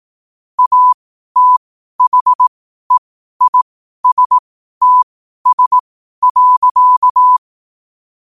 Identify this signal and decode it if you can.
Morse code, heard as a keyed tone, about 18 wpm, spaced as for 12 wpm, 1000 Hz. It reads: ATHEISTS.
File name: qotdfiles / 12wpm